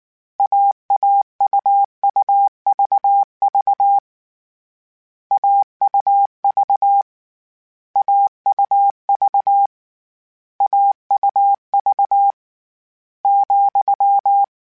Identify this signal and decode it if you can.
Morse code, heard as a keyed tone, about 19 wpm, 800 Hz. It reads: AAUUVV AUV AUV AUV ,
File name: Difficult Combos / Difficult Combos-mp3s / AUV AUV4 @18